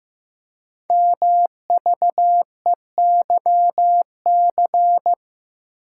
Morse code, heard as a keyed tone, about 15 wpm, 700 Hz. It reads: MVEYC
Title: Code Groups - 5 Letters - 15wpm